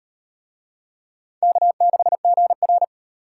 Morse code, heard as a keyed tone, about 38 wpm, 700 Hz. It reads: K6GR